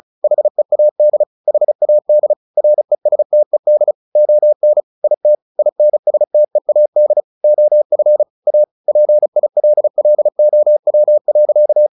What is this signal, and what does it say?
Morse code, heard as a keyed tone, about 34 wpm, 600 Hz. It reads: HEAD HAD RESTED ON IT INSTEAD OF A PILLOW.
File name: the old man and the sea 2 35wpm